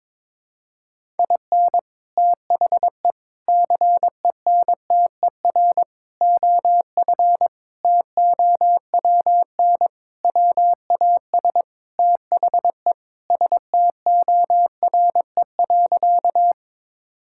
Morse code, heard as a keyed tone, about 22 wpm, 700 Hz. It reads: IN THE CENTER OF TOWN WAS THE STORE.
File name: practice / 22wpm / Sentences with 4-Word Phrases - 22wpm